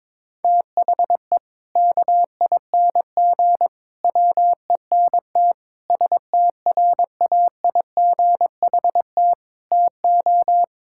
Morse code, heard as a keyed tone, about 22 wpm, 700 Hz. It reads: THE KING WENT STRAIGHT TO
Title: Sentences from Top 1000 Words - Review Part 6 - 22wpm